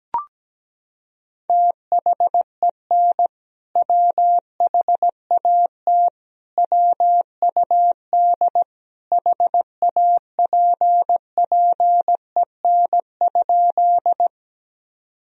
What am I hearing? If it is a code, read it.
Morse code, heard as a keyed tone, about 17 wpm, 700 Hz. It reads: THEN WHAT WUD HAPPEN?
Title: Sentences from Top 400 Words and Abbreviations 17wpm